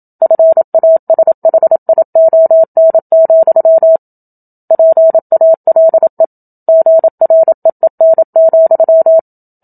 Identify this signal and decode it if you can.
Morse code, heard as a keyed tone, about 27 wpm, 650 Hz. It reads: FASHION, PALE GREEN,
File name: Book Two / 01-Chapter 1 - 27 WPM